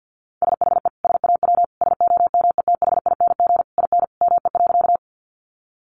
Morse code, heard as a keyed tone, about 50 wpm, 700 Hz. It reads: SHE SAW SOMETHING IN ME.
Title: Sentences from Top 600 Words 50wpm